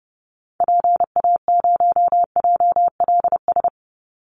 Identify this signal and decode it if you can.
Morse code, heard as a keyed tone, about 30 wpm, 700 Hz. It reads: PA0JLS
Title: Call Signs Repeated 3x - 30wpm